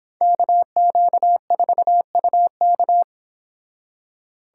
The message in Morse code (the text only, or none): KQ4UK